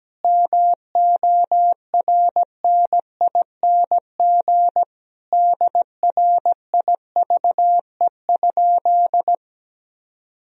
Morse code, heard as a keyed tone, about 17 wpm, 700 Hz. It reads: MORNING DRIVE?
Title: Sentences from Top 1000 Words - Review Part 3 - 17wpm